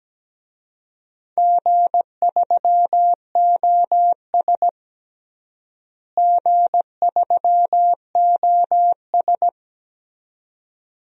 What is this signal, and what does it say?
Morse code, heard as a keyed tone, about 17 wpm, 700 Hz. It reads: G3OS G3OS
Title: DXCC-100 Call Signs - 4 Characters - Repeated 2x - 17wpm